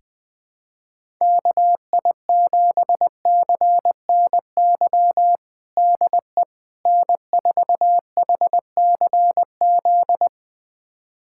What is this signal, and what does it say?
Morse code, heard as a keyed tone, about 20 wpm, 700 Hz. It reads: KI7CNY DE N4HCZ
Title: USA Callsign DE Callsign 20wpm